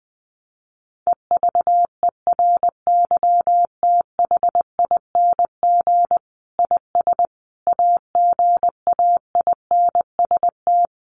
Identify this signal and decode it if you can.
Morse code, heard as a keyed tone, about 20 wpm, 700 Hz. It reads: EVERYTHING IS AGAINST
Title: Sentences from Top 700 Words 20wpm